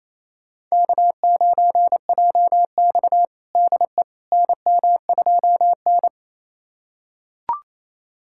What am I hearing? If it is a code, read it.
Morse code, heard as a keyed tone, about 28 wpm, 700 Hz. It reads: K9JX DE NM2N
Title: USA Callsign DE Callsign - 4 Characters - 28wpm